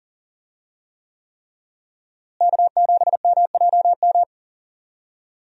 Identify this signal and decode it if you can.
Morse code, heard as a keyed tone, about 40 wpm, 700 Hz. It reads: K7MJM
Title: USA Call Signs - 5 Characters - Repeated 3x - 40wpm